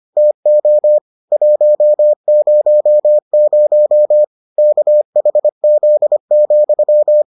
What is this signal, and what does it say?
Morse code, heard as a keyed tone, about 25 wpm, 600 Hz. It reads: TO 100 KHZ,